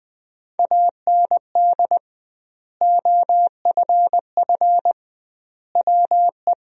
Morse code, heard as a keyed tone, about 20 wpm, 700 Hz. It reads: AND OFF WE
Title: 2x Word-Spacing - Sentences from Top 200 Words - 20wpm